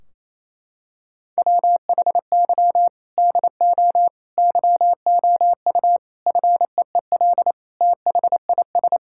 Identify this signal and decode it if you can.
Morse code, heard as a keyed tone, about 28 wpm, 700 Hz. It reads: WHY DO YOU FEEL THIS
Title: Sentences from Top 1600 Words 28wpm